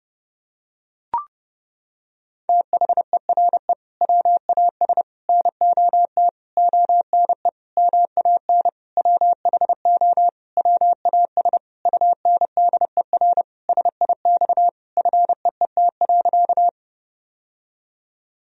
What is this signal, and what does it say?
Morse code, heard as a keyed tone, about 30 wpm, 700 Hz. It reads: THERE WAS NOT ONE MAN WHO WAS UNDER SIX FEET.
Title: Sentences from Top 1000 Words - Review Part 2 - 30wpm